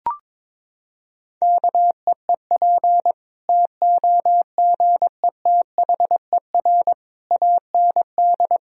Morse code, heard as a keyed tone, about 22 wpm, 700 Hz. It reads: KEEP TOGETHER AND
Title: Sentences from Top 1000 Words - Review Part 1 - 22wpm